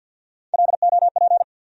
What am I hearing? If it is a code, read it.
Morse code, heard as a keyed tone, about 52 wpm, 690 Hz. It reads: LOP
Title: USA Call Sign Postfixes - 3 Letters 50wpm